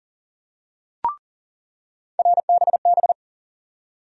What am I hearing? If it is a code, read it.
Morse code, heard as a keyed tone, about 40 wpm, 700 Hz. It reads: RBB
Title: USA Call Sign Postfixes - 3 Letters 40wpm